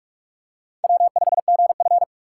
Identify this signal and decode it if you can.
Morse code, heard as a keyed tone, about 40 wpm, 700 Hz. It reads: W5GF